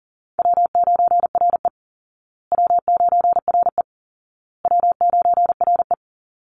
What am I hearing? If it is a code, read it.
Morse code, heard as a keyed tone, about 40 wpm, 700 Hz. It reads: W9RE W9RE W9RE
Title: Call Signs Repeated 2x - 40wpm